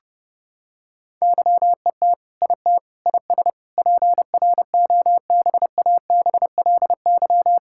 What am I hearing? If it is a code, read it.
Morse code, heard as a keyed tone, about 30 wpm, 700 Hz. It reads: YET IT IS PROBABLY
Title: Sentences from Top 1600 Words 30wpm